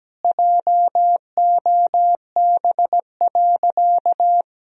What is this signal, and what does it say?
Morse code, heard as a keyed tone, about 17 wpm, 700 Hz. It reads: JOB.
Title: Sentences from Top 1000 Words - Review Part 3 - 17wpm